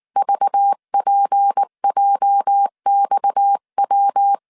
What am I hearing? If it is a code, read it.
Morse code, heard as a keyed tone, about 19 wpm, 800 Hz. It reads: VPJXW
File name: CW / DET / 1 9 1500C S C 18wpm 2539